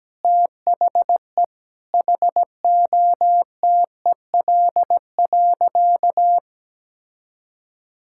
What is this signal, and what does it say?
Morse code, heard as a keyed tone, about 17 wpm, 700 Hz. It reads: THE HOTEL.
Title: Sentences from Top 1600 Words 17wpm